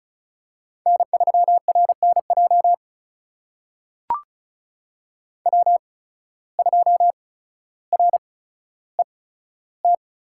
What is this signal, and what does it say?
Morse code, heard as a keyed tone, about 36 wpm, spaced as for 15 wpm, 700 Hz. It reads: N3RNJ W2RET
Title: USA Call Signs - 5 Characters - 8x Character Spacing - 35wpm